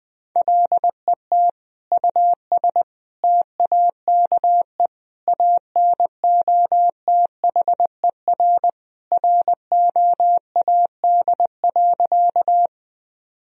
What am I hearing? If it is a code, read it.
Morse code, heard as a keyed tone, about 20 wpm, 700 Hz. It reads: LET US TAKE ANOTHER ROAD.
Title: Sentences from Top 1000 Words - Review Part 2 - 20wpm